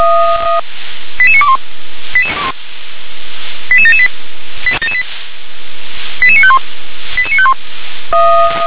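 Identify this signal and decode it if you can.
Five-tone selective call (ZVEI), heard as quick runs of five tones, tones of about 70 ms. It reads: 88921 88778 88951 88951
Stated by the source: ZVEI five-tone call received by SDR.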